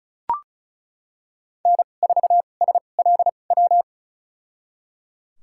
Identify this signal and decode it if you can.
Morse code, heard as a keyed tone, about 34 wpm, spaced as for 25 wpm, 700 Hz. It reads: N4SLW
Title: USA Call Signs - 2x Character Spacing - 35wpm